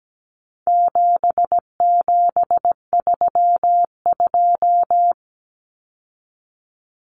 Morse code, heard as a keyed tone, about 17 wpm, 700 Hz. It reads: 7732
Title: Common Cut Numbers - 4-Digit 17wpm